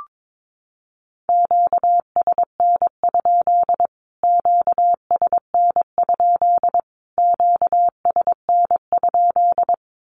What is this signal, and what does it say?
Morse code, heard as a keyed tone, about 22 wpm, 700 Hz. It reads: QSN? QSN? QSN?
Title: Q-Codes Repeated 3x 22wpm